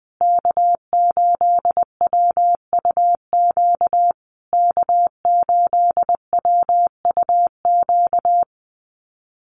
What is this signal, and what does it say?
Morse code, heard as a keyed tone, about 20 wpm, 700 Hz. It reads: K8WUQ K8WUQ
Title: USA Call Signs Repeated 3x 20wpm